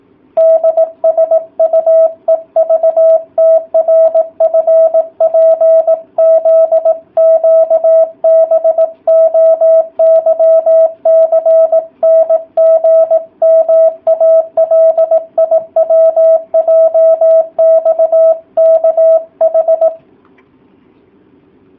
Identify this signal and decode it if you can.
Morse code, heard as a keyed tone, about 18 wpm, 650 Hz. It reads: DSUEVTRFPZQBOYCNGMALIWJXKH